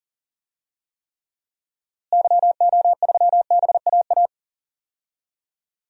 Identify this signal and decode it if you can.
Morse code, heard as a keyed tone, about 37 wpm, 700 Hz. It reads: YO3BAA
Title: DXCC-100 Call Signs - 6 Characters - Repeated 3x - 40wpm